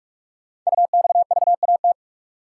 Morse code, heard as a keyed tone, about 44 wpm, 690 Hz. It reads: UXVAT
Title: Code Groups - 5 Letters - 45wpm